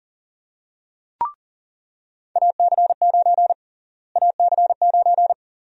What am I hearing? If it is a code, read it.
Morse code, heard as a keyed tone, about 40 wpm, 700 Hz. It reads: AC9 AC9
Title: USA Call Sign Prefixes - Repeated 2x - 40wpm